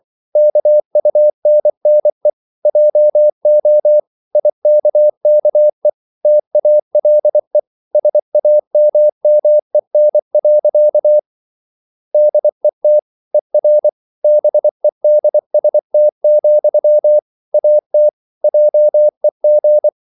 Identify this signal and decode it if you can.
Morse code, heard as a keyed tone, about 24 wpm, 600 Hz. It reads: KUNNE JO IKKE TALE SAMMEN. DET ER BEDST, AT JEG
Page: Ferie 24 wpm